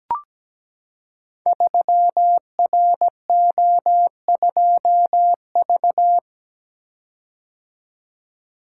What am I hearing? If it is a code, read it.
Morse code, heard as a keyed tone, about 17 wpm, 700 Hz. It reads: 3RO2V
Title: Code Groups - 5 Letter:Numbers - 17wpm